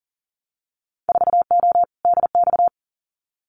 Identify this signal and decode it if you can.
Morse code, heard as a keyed tone, about 42 wpm, 700 Hz. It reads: 4O DX
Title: Winter Field Day Exchanges Encore 40wpm